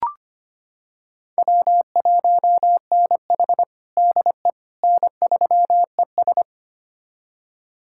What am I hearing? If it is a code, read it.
Morse code, heard as a keyed tone, about 25 wpm, 700 Hz. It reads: W1NH DE N3ES
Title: USA Callsign DE Callsign - 4 Characters - 25wpm